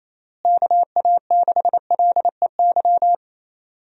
Morse code, heard as a keyed tone, about 27 wpm, 700 Hz. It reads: KA6LEY
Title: USA Call Signs - 6 Characters - Repeated 2x - 28wpm